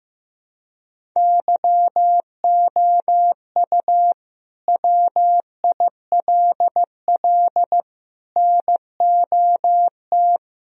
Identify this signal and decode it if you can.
Morse code, heard as a keyed tone, about 15 wpm, 700 Hz. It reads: YOU WILL NOT
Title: Sentences from Top 1000 Words - Review Part 1 - 15wpm